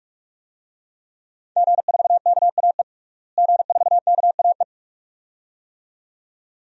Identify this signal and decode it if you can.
Morse code, heard as a keyed tone, about 41 wpm, 700 Hz. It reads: G4KAE G4KAE
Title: DXCC-100 Call Signs - 5 Characters - Repeated 2x - 45wpm